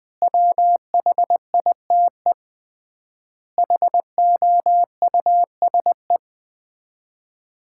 Morse code, heard as a keyed tone, about 21 wpm, 700 Hz. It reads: WHITE HOUSE
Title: 3x Word-Spacing - Sets of 4 Words - 20wpm